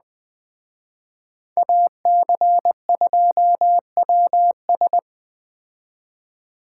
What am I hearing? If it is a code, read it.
Morse code, heard as a keyed tone, about 20 wpm, 700 Hz. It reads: AC2WS